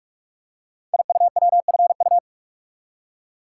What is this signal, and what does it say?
Morse code, heard as a keyed tone, about 44 wpm, 700 Hz. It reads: IUWFU
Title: Code Groups - 5 Letters - 45wpm